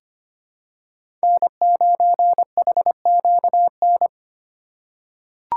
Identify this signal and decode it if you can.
Morse code, heard as a keyed tone, about 25 wpm, 700 Hz. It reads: N9HQN
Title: USA Call Signs - 5 Characters - 4x Character Spacing - 25wpm